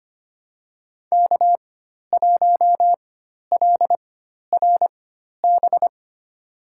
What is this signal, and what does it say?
Morse code, heard as a keyed tone, about 25 wpm, spaced as for 12 wpm, 700 Hz. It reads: K1LRB
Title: USA Call Signs - 5 Characters - 4x Character Spacing - 25wpm